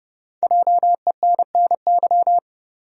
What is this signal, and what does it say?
Morse code, heard as a keyed tone, about 30 wpm, 700 Hz. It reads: JENNY
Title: USA Common Names - Max 5 Letters - 30wpm